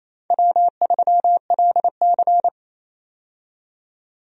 Morse code, heard as a keyed tone, about 28 wpm, 700 Hz. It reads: W3LC
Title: USA Call Signs - 4 Characters - 28wpm